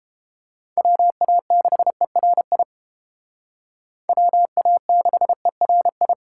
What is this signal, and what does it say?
Morse code, heard as a keyed tone, about 29 wpm, 700 Hz. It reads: WA6ERI WA6ERI